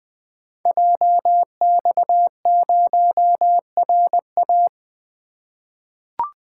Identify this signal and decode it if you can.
Morse code, heard as a keyed tone, about 21 wpm, 700 Hz. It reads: JX0RA
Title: Code Groups - 5 Letter:Numbers - 20wpm